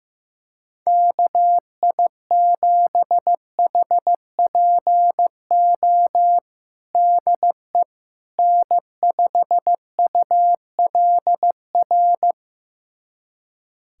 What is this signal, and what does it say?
Morse code, heard as a keyed tone, about 15 wpm, 700 Hz. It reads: KI7HPO DE N5ULR
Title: USA Callsign DE Callsign 15wpm